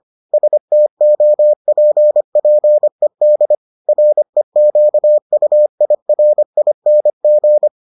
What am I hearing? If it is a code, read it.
Morse code, heard as a keyed tone, about 25 wpm, 600 Hz. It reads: STOPPED REQUIRING